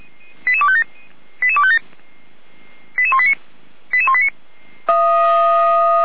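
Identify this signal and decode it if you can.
Five-tone selective call (ZVEI), heard as quick runs of five tones, tones of about 70 ms. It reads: 88257 88257 88189 88189
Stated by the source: ZVEI five-tone call 88122 received with an SDR.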